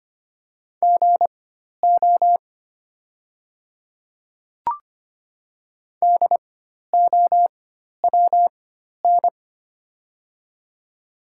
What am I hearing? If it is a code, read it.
Morse code, heard as a keyed tone, about 25 wpm, spaced as for 12 wpm, 700 Hz. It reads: GO DOWN
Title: 4x Character Spacing - Top 100 Words - 25wpm